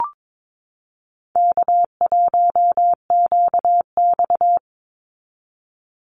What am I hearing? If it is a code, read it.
Morse code, heard as a keyed tone, about 22 wpm, 700 Hz. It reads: K1QX
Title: Easy Call Signs Repeated 3x - 22wpm